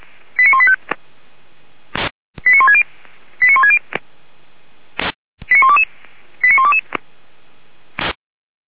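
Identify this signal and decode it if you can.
Five-tone selective call (ZVEI), heard as quick runs of five tones, tones of about 70 ms. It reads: 89185 89160 89160 89122 89122
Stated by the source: ZVEI five-tone call received by SDR, feuerwehr-orlamuende.org (fire brigade, Orlamünde).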